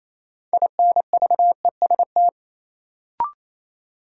Morse code, heard as a keyed tone, about 28 wpm, 700 Hz. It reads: INVEST
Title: Sentences from Top 1100 Words - New Word Drill Alt - 28wpm